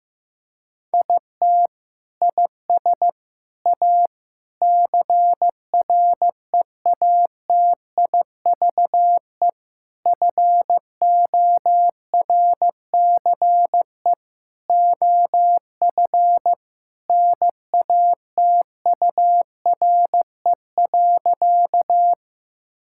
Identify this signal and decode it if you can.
Morse code, heard as a keyed tone, about 15 wpm, 700 Hz. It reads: IT IS A CREATIVE FORCE OF NATURE.
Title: Sentences from Top 1000 Words - Review Part 5 - 15wpm